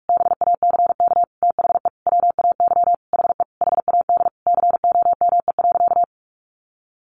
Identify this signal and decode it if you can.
Morse code, heard as a keyed tone, about 44 wpm, 700 Hz. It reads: BACK THE WAY HE HAD COME.
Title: Sentences from Top 1100 Words 45wpm